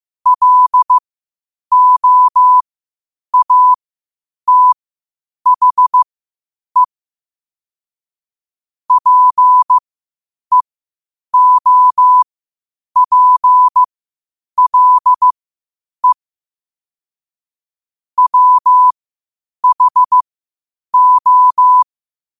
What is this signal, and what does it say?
Morse code, heard as a keyed tone, about 15 wpm, spaced as for 8 wpm, 1000 Hz. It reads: LOATHE PEOPLE WHO
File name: qotdfiles / 8wpm